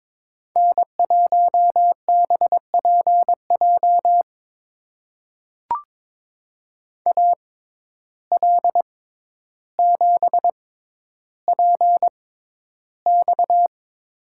Morse code, heard as a keyed tone, about 22 wpm, spaced as for 11 wpm, 700 Hz. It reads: N1BPJ AL7PX
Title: USA Call Signs - 5 Characters - 6x Character Spacing - 22wpm